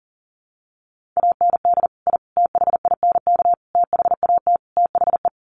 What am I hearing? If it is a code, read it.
Morse code, heard as a keyed tone, about 40 wpm, 700 Hz. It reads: AND I THINK THAT THE
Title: Sets of 5 Words 40wpm